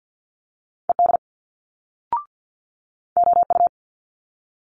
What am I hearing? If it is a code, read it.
Morse code, heard as a keyed tone, about 50 wpm, 700 Hz. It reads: ED OU
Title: Multi-letter Phonemes Quick Fire 50wpm